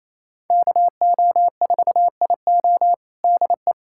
Morse code, heard as a keyed tone, about 27 wpm, 700 Hz. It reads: KO4IO DE
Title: USA Callsign DE Callsign 28wpm